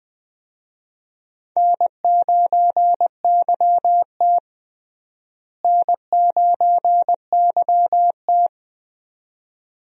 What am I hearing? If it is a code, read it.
Morse code, heard as a keyed tone, about 20 wpm, 700 Hz. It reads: N9YT N9YT
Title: USA Call Signs - 4 Characters - Repeated 2x - 20wpm